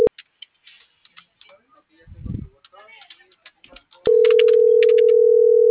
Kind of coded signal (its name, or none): telephone call progress tone